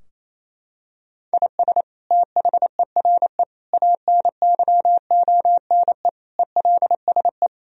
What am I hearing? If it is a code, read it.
Morse code, heard as a keyed tone, about 28 wpm, 700 Hz. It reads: IS THERE ANYONE ELSE
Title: Sentences from Top 1000 Words - Review Part 3 - 28wpm